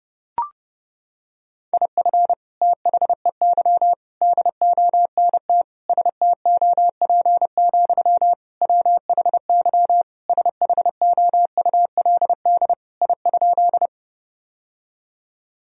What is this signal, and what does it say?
Morse code, heard as a keyed tone, about 30 wpm, 700 Hz. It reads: IF THEY DONT STOP, WHY SHOULD I?
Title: Sentences from Top 1000 Words - Review Part 1 - 30wpm